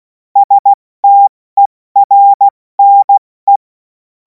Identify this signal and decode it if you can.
Morse code, heard as a keyed tone, about 16 wpm, 800 Hz. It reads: STERNE
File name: Morsekurs / Witze / Witz 24. Februar 2026 70 F80